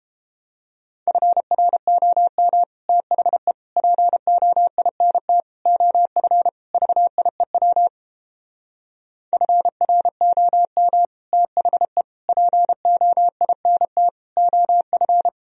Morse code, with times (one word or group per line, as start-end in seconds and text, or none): 1.1-2.6 FROM
2.9-3.5 THE
3.8-5.4 POINT
5.7-6.5 OF
6.7-7.9 VIEW
9.3-11.1 FROM
11.3-12.0 THE
12.3-14.1 POINT
14.4-15.3 OF